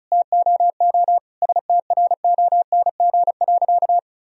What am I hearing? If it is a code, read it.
Morse code, heard as a keyed tone, about 36 wpm, 700 Hz. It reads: TOO STRONG.
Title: Sentences from Top 1000 Words - Review Part 4 - 35wpm